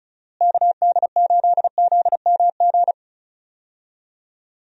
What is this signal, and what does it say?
Morse code, heard as a keyed tone, about 36 wpm, 700 Hz. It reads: KD8ZMG
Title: USA Call Signs - 6 Characters - Repeated 2x - 35wpm